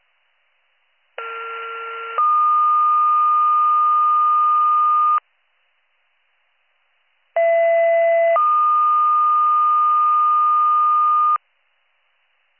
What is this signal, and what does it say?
Two-tone page, heard as a one-second tone, then a three-second tone, about 1370 then 1220 Hz, about 670 then 1220 Hz. It is heard from 1.2 to 11.4 s.